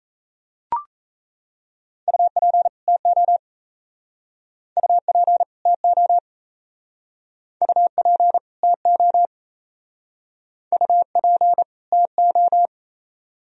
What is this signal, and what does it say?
Morse code, heard as a keyed tone, about 38 wpm, 700 Hz. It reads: UP TO UP TO UP TO UP TO